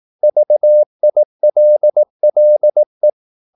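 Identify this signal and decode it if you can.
Morse code, heard as a keyed tone, about 18 wpm, 600 Hz. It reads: VILLE
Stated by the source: Slow (QRS) Morse practice at 18 wpm, CW med Gnister.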